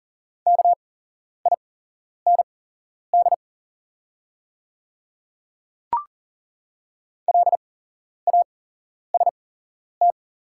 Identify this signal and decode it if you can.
Morse code, heard as a keyed tone, about 40 wpm, spaced as for 11 wpm, 690 Hz. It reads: KIND LAST
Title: 8x Character Spacing - Top 300 Words - 40wpm